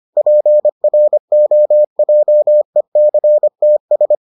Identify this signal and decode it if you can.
Morse code, heard as a keyed tone, about 25 wpm, 600 Hz. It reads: PROJECTS